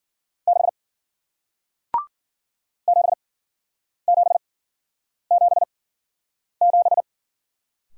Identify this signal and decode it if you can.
Morse code, heard as a keyed tone, about 48 wpm, spaced as for 9 wpm, 700 Hz. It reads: 6 7777
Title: Speed-Racing Single Letter-Number 40wpm